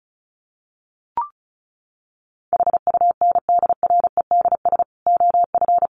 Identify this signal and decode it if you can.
Morse code, heard as a keyed tone, about 34 wpm, 700 Hz. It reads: HUNDREDS OF
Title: Sets of 5 Words 35wpm